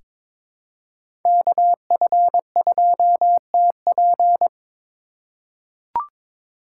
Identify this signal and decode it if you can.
Morse code, heard as a keyed tone, about 22 wpm, 700 Hz. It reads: KF2TP
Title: K1USN SST QSO Practice - 22wpm